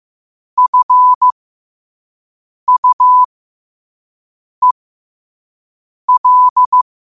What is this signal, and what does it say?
Morse code, heard as a keyed tone, about 15 wpm, spaced as for 5 wpm, 1000 Hz. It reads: FUEL
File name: qotdfiles / 5wpm